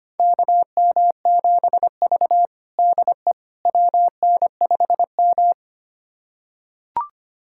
Morse code, heard as a keyed tone, about 25 wpm, 700 Hz. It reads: KM7V DE WN5M
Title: USA Callsign DE Callsign - 4 Characters - 25wpm